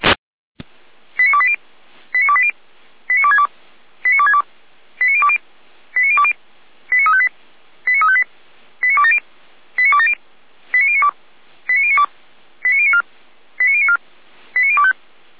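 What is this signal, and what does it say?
Five-tone selective call (ZVEI), heard as quick runs of five tones, tones of about 70 ms. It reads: 89280 89280 89262 89262 89020 89020 89358 89358 89270 89270 89092 89092 89094 89094 89025
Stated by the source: ZVEI five-tone selective call 89330, received by SDR.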